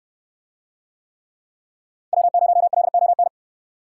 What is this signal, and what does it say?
Morse code, heard as a keyed tone, about 65 wpm, 700 Hz. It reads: W0LON